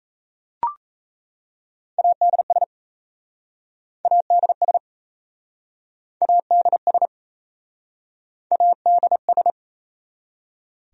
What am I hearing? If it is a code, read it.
Morse code, heard as a keyed tone, about 37 wpm, 700 Hz. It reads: ADS ADS ADS ADS